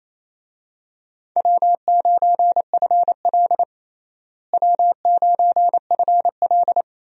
Morse code, heard as a keyed tone, about 28 wpm, 700 Hz. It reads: W9FL W9FL
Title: USA Call Signs - 4 Characters - Repeated 2x - 28wpm